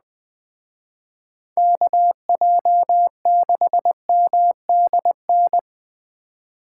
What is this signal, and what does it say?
Morse code, heard as a keyed tone, about 20 wpm, 700 Hz. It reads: KJ6MDN
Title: Speed Racing - USA Call Signs 17wpm